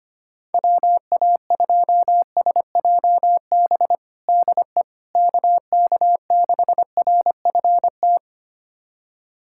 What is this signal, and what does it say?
Morse code, heard as a keyed tone, about 25 wpm, 700 Hz. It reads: WA2SJB DE KK6RFT
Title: USA Callsign DE Callsign - 6 Characters - 25wpm